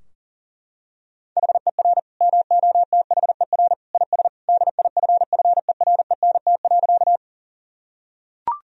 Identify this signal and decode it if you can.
Morse code, heard as a keyed tone, about 40 wpm, 700 Hz. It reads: HER MOTHER IS DIFFERENT.
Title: Sentences from Top 1000 Words - Review Part 3 - 40wpm